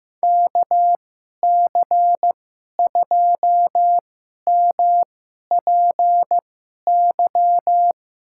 Morse code, heard as a keyed tone, about 15 wpm, spaced as for 11 wpm, 700 Hz. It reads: KC2MPY
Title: USA Call Signs - 2x Character Spacing - 15wpm